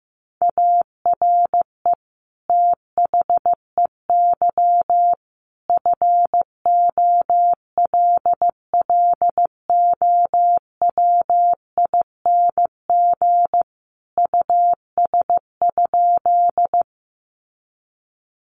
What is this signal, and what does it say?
Morse code, heard as a keyed tone, about 15 wpm, 700 Hz. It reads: ARE THEY FOLLOWING US?
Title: Sentences from Top 700 Words 15wpm